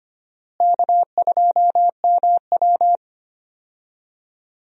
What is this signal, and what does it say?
Morse code, heard as a keyed tone, about 25 wpm, 700 Hz. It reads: K2MW
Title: USA Call Signs - 4 Characters - Repeated 2x - 25wpm